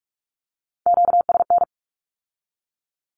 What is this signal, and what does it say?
Morse code, heard as a keyed tone, about 46 wpm, 700 Hz. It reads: QSN
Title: Q-Codes with Spoken Meaning - 45wpm